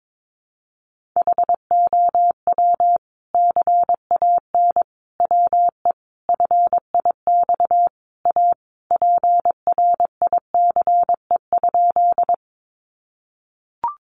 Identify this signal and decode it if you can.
Morse code, heard as a keyed tone, about 22 wpm, 700 Hz. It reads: HOW CAN WE FIX A PRICE?
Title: Sentences from Top 1400 Words 22wpm